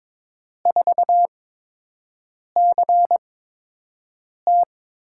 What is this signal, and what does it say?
Morse code, heard as a keyed tone, about 22 wpm, spaced as for 6 wpm, 700 Hz. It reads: 4CT
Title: USA Call Signs - 5 Characters - 8x Character Spacing - 22wpm